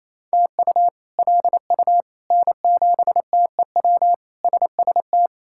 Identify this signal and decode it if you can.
Morse code, heard as a keyed tone, about 27 wpm, 700 Hz. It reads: TU LU N7TEW SST
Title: K1USN SST QSO Practice - 28wpm